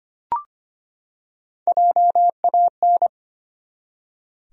Morse code, heard as a keyed tone, about 25 wpm, 700 Hz. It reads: JAN
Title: USA Common Names - Max 3 Letters - 25wpm